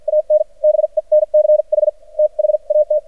Morse code, heard as a keyed tone, about 53 wpm, spaced as for 42 wpm, 610 Hz. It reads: AN BENCH THAT
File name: archived / CRSnet